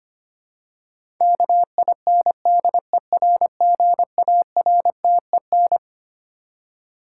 Morse code, heard as a keyed tone, about 25 wpm, 700 Hz. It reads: KINDERGARTEN
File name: Twelve Letter Words - 25wpm